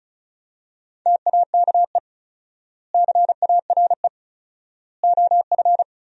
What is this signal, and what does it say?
Morse code, heard as a keyed tone, about 34 wpm, 700 Hz. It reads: TAKE CARE OF
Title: 4x Word-Spacing - Sentences from Top 300 Words - 35wpm